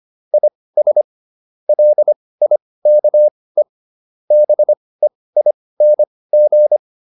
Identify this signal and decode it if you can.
Morse code, heard as a keyed tone, about 25 wpm, spaced as for 18 wpm, 600 Hz. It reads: IS LIKE BEING